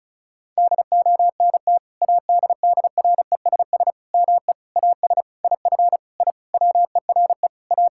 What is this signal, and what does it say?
Morse code, heard as a keyed tone, about 36 wpm, 700 Hz. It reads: DONT ADDRESS ME AS IF I WERE A
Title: Sentences from Top 1000 Words - Review Part 6 - 35wpm